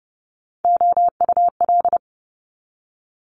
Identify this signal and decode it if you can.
Morse code, heard as a keyed tone, about 30 wpm, 700 Hz. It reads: OUL
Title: Multi-letter Phonemes 30wpm